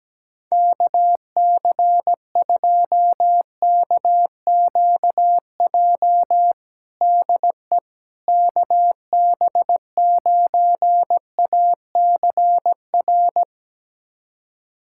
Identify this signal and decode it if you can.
Morse code, heard as a keyed tone, about 17 wpm, 700 Hz. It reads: KC2KQJ DE KB9ACR
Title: USA Callsign DE Callsign - 6 Characters - 17wpm